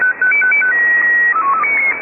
SSTV (slow-scan television), Scottie 1.
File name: items / Httparchive.orgdetailsShortwavePirateRadioCont2 / AmphetamineRadio6925Usb2255z-2337z11-19-15